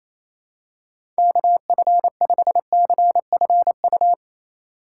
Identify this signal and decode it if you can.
Morse code, heard as a keyed tone, about 28 wpm, 700 Hz. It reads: KF5CFU
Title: USA Call Signs - 6 Characters - 3x Character Spacing - 28wpm